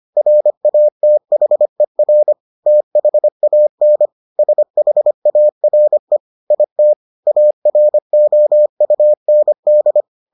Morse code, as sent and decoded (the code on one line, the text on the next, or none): .-. .- - .... .-. / - .... .- -. / ... .... .- .-. / .. - / .- .-. --- ..- -. -..
RATHER THAN SHARE IT AROUND